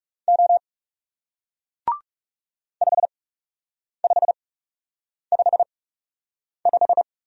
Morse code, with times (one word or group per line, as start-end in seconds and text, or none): 0.3-0.6 O
2.8-7.0 5555